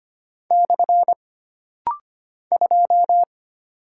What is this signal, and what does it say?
Morse code, heard as a keyed tone, about 25 wpm, 700 Hz. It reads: / 2
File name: Single Character (Letter:Number:Punctuation:Prosign) - Rapid-Fire - 25wpm